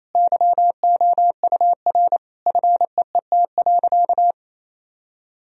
Morse code, heard as a keyed tone, about 28 wpm, 700 Hz. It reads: YOUR FEET.
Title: Sentences from Top 1000 Words - Review Part 1 - 28wpm